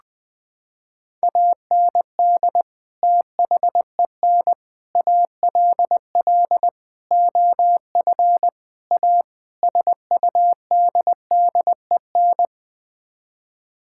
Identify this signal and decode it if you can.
Morse code, heard as a keyed tone, about 20 wpm, 700 Hz. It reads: AND THEN ALL OF A SUDDEN
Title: Sets of 6 Words - 20wpm